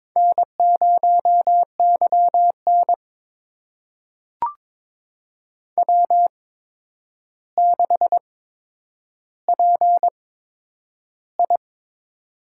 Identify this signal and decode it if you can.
Morse code, heard as a keyed tone, about 22 wpm, spaced as for 9 wpm, 700 Hz. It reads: N0YN W6PI
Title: USA Call Signs - 4 Characters - 8x Character Spacing - 22wpm